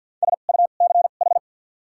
Morse code, heard as a keyed tone, about 53 wpm, spaced as for 37 wpm, 700 Hz. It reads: SUCH
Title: 2x Character Spacing - Top 200 Words - 50wpm